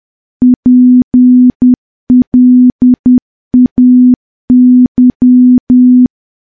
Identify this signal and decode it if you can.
Morse code, heard as a keyed tone, about 10 wpm, 250 Hz. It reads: PLAY
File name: morse / morse-training-10WPM